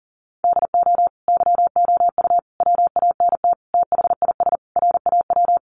Morse code, heard as a keyed tone, about 40 wpm, 700 Hz. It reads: DO YOU WANT THIS RAW
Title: Sentences from Top 1500 Words 40wpm